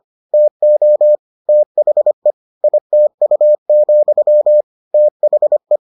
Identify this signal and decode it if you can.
Morse code, heard as a keyed tone, about 25 wpm, 600 Hz. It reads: TO THE ITU, THE